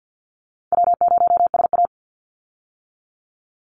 Morse code, heard as a keyed tone, about 50 wpm, 700 Hz. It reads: W0SA
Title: Easy Call Signs - 50wpm